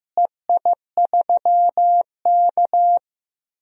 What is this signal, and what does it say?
Morse code, heard as a keyed tone, about 15 wpm, 700 Hz. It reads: EI3K
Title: DXCC-100 Call Signs - 4 Characters - Repeated 2x - 15wpm